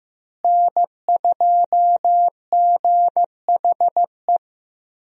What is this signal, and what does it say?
Morse code, heard as a keyed tone, about 15 wpm, 700 Hz. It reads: N2GHE
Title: USA Call Signs - 5 Characters - Repeated 3x - 15wpm